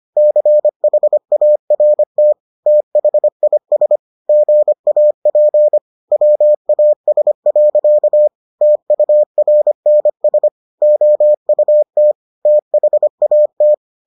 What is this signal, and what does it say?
Morse code, heard as a keyed tone, about 25 wpm, 600 Hz. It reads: CHART THIS GAP WAS. TURNS OUT THAT